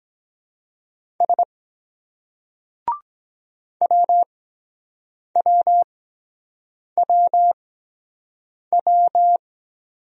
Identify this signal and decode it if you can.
Morse code, heard as a keyed tone, about 23 wpm, spaced as for 7 wpm, 700 Hz. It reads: S WWWW